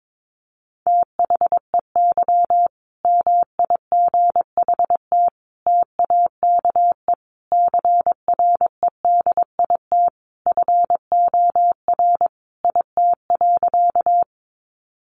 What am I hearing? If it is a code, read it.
Morse code, heard as a keyed tone, about 22 wpm, 700 Hz. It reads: THEY MIGHT TAKE CREDIT FOR IT.
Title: Sentences from Top 1000 Words 22wpm